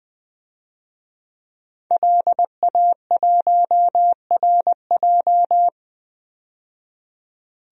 Morse code, heard as a keyed tone, about 20 wpm, 700 Hz. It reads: LA1RJ
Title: DXCC-100 Call Signs - 5 Characters - 20wpm